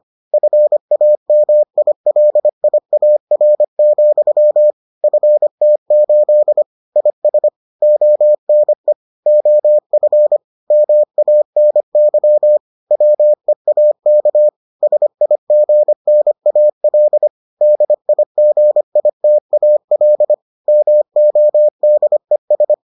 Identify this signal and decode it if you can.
Morse code, heard as a keyed tone, about 25 wpm, 600 Hz. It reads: FAMILIAR, FT8 IS ONE OF MANY WEAK SIGNAL DIGITAL MODES